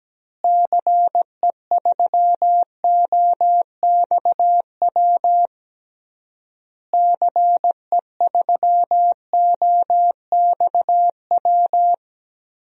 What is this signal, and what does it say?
Morse code, heard as a keyed tone, about 17 wpm, 700 Hz. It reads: CE3OXW CE3OXW
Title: DXCC-100 Call Signs - 6 Characters - Repeated 3x - 17wpm